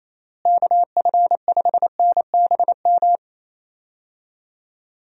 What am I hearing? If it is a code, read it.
Morse code, heard as a keyed tone, about 28 wpm, 700 Hz. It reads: KF5NBM
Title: USA Call Signs - 6 Characters - Repeated 2x - 28wpm